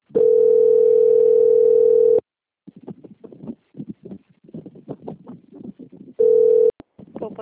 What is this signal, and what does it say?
Ringback tone (North America), 440 + 480 Hz.
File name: out- - - - - .26300